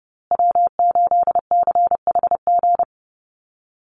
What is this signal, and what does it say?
Morse code, heard as a keyed tone, about 30 wpm, 700 Hz. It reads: W8CHG
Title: USA Call Signs Repeated 3x 30wpm